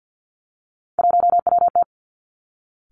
Morse code, heard as a keyed tone, about 50 wpm, 700 Hz. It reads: 1WT
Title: Three Challenging Letter-Number Combinations 50wpm